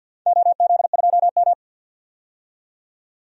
Morse code, heard as a keyed tone, about 52 wpm, 700 Hz. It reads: OZ2M
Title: DXCC-100 Call Signs - 4 Characters - 50wpm